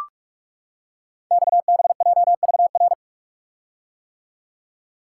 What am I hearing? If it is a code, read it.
Morse code, heard as a keyed tone, about 44 wpm, 700 Hz. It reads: XBJVR
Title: Code Groups - 5 Letters - 45wpm